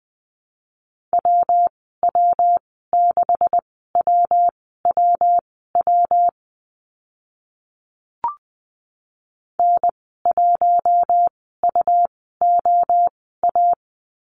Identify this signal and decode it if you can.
Morse code, heard as a keyed tone, about 20 wpm, spaced as for 14 wpm, 700 Hz. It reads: WW6WWW N1UOA
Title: 140 Lesson 23g - Letter f - Call Signs 20wpm